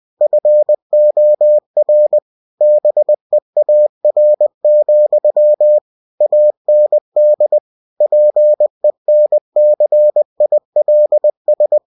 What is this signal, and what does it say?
Morse code, heard as a keyed tone, about 20 wpm, 600 Hz. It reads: FOR BEAR, AND PENCILS
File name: cw-books / winnie-the-pooh / winnie-20wpm-page-0109